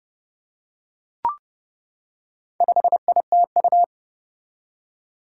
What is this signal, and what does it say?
Morse code, heard as a keyed tone, about 30 wpm, 700 Hz. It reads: 5ITU
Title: Four Challenging Letter-Number Combinations 30wpm